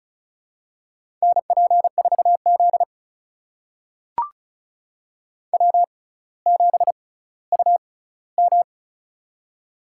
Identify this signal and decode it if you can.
Morse code, heard as a keyed tone, about 34 wpm, spaced as for 18 wpm, 700 Hz. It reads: NP4Z W7UM
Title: USA Call Signs - 4 Characters - 6x Character Spacing - 35wpm